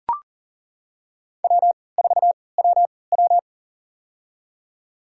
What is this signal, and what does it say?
Morse code, heard as a keyed tone, about 39 wpm, spaced as for 22 wpm, 700 Hz. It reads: W4WW
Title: USA Call Signs - 4 Characters - 3x Character Spacing - 40wpm